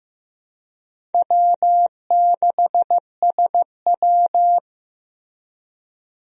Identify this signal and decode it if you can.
Morse code, heard as a keyed tone, about 15 wpm, 700 Hz. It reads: W6SW
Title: USA Call Signs - 4 Characters - Repeated 2x - 15wpm